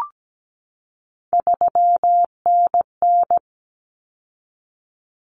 Morse code, heard as a keyed tone, about 17 wpm, 700 Hz. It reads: 3NN
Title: Common Cut Numbers - 3-Digit 17wpm